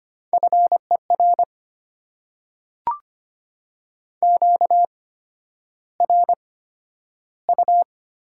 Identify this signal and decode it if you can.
Morse code, heard as a keyed tone, about 25 wpm, spaced as for 11 wpm, 690 Hz. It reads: FER QRU
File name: Top 90 QSO Elements - 8x Character Spacing - 25wpm